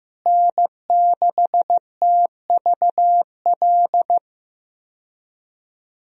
Morse code, heard as a keyed tone, about 15 wpm, 700 Hz. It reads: N6TVL
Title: USA Call Signs - 5 Characters - Repeated 2x - 15wpm